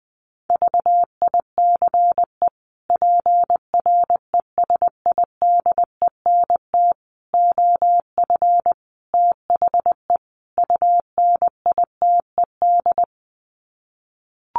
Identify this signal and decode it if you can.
Morse code, heard as a keyed tone, about 20 wpm, 700 Hz. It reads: VICE PRESIDENT OF THE UNITED
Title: Sets of 5 Words Encore2 20wpm